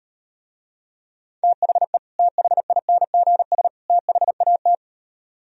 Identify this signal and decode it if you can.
Morse code, heard as a keyed tone, about 38 wpm, 700 Hz. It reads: THE THINGS THAT